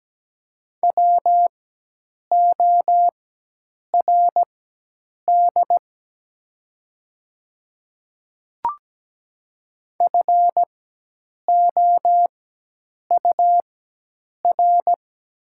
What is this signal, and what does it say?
Morse code, heard as a keyed tone, about 17 wpm, spaced as for 8 wpm, 700 Hz. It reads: WORD FOUR
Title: 4x Character Spacing - Top 300 Words - 17wpm